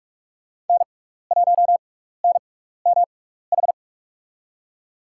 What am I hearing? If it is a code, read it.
Morse code, heard as a keyed tone, about 44 wpm, spaced as for 15 wpm, 700 Hz. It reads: N1NMH
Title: USA Call Signs - 5 Characters - 6x Character Spacing - 45wpm